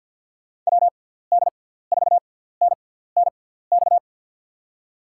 Morse code, heard as a keyed tone, about 52 wpm, spaced as for 17 wpm, 700 Hz. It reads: WD4NNX
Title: USA Call Signs - 6x Character Spacing - 50wpm